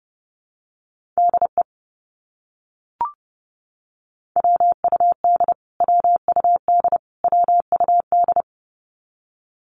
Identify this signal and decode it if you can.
Morse code, heard as a keyed tone, about 30 wpm, 700 Hz. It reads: DE WUD WUD WUD
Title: Abbreviations Repeated 3x - 30wpm